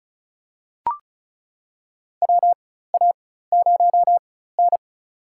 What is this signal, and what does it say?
Morse code, heard as a keyed tone, about 34 wpm, spaced as for 16 wpm, 700 Hz. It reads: WA0N
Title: USA Call Signs - 4 Characters - 4x Character Spacing - 35wpm